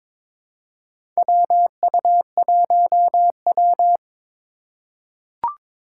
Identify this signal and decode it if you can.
Morse code, heard as a keyed tone, about 22 wpm, 700 Hz. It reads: WU1W
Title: USA Call Signs - 4 Characters - 4x Character Spacing - 22wpm